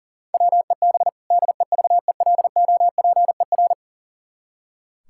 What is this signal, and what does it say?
Morse code, heard as a keyed tone, about 40 wpm, 700 Hz. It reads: WEB DEVELOPER
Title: K1USN SST QSO Practice 40wpm